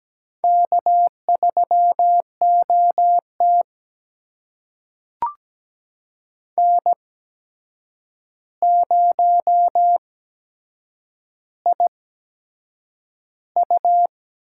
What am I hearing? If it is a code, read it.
Morse code, heard as a keyed tone, about 17 wpm, spaced as for 7 wpm, 700 Hz. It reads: K3OT N0IU
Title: USA Call Signs - 8x Character Spacing - 17wpm